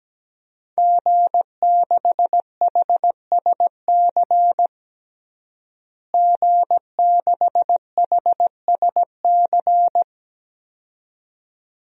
Morse code, heard as a keyed tone, about 17 wpm, 700 Hz. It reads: G6HSC G6HSC
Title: DXCC-100 Call Signs - 5 Characters - Repeated 2x - 17wpm